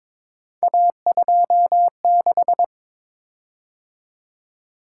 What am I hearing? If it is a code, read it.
Morse code, heard as a keyed tone, about 22 wpm, 700 Hz. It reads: A26
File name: practice / 22wpm / Three Letter-Number Combinations - 22wpm